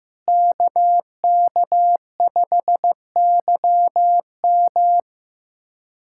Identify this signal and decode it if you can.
Morse code, heard as a keyed tone, about 16 wpm, 700 Hz. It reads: KK5YM
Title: USA Callsign DE Callsign - 5 Characters - 15wpm